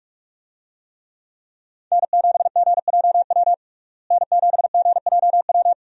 Morse code, heard as a keyed tone, about 48 wpm, 700 Hz. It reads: N7GJW N7GJW
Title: USA Call Signs - 5 Characters - Repeated 2x - 45wpm